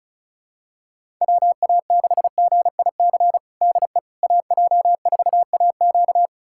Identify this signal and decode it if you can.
Morse code, heard as a keyed tone, about 34 wpm, 700 Hz. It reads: WA6GIC DE AJ4AQ
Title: USA Callsign DE Callsign 35wpm